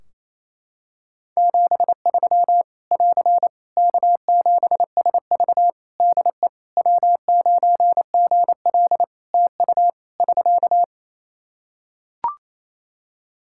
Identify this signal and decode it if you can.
Morse code, heard as a keyed tone, about 28 wpm, 700 Hz. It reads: 73 <AR> K7SV DE W9GL TU <SK>
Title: Practice QSO - Exchange 4 - Ending - 28wpm